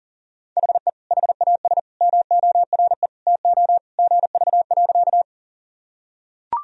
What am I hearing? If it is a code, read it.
Morse code, heard as a keyed tone, about 37 wpm, 700 Hz. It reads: HE HAS MORE TO GV.
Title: Sentences from Top 200 Words and Abbreviations 40wpm